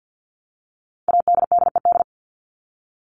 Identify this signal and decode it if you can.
Morse code, heard as a keyed tone, about 50 wpm, 700 Hz. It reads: ADDED
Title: ED Suffix - Top 100 Words with Base Words 50wpm